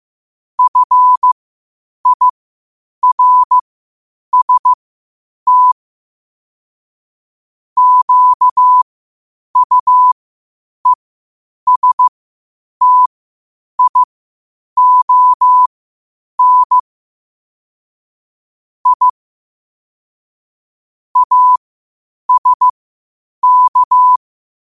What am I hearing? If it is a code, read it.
Morse code, heard as a keyed tone, about 15 wpm, spaced as for 8 wpm, 1000 Hz. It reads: FIRST QUESTION I ASK